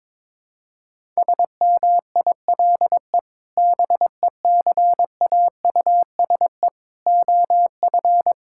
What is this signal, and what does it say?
Morse code, heard as a keyed tone, about 22 wpm, 700 Hz. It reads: SMILE BECAUSE OF
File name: practice / 22wpm / Sentences with 2-Word Phrases - 22wpm